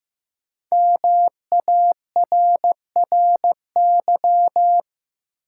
Morse code, heard as a keyed tone, about 16 wpm, 700 Hz. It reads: MARRY
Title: Sentences from Top 1200 Words - New Word Drill - 15wpm